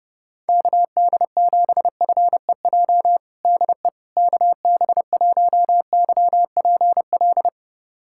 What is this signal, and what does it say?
Morse code, heard as a keyed tone, about 30 wpm, 700 Hz. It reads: KD7FEJ DE KB1YPL
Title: USA Callsign DE Callsign - 6 Characters - 30wpm